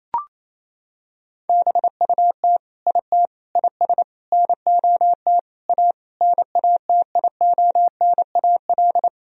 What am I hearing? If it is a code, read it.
Morse code, heard as a keyed tone, about 28 wpm, 700 Hz. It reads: BUT IT IS NOT A NATIONAL
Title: Sentences from Top 1000 Words - Review Part 4 - 28wpm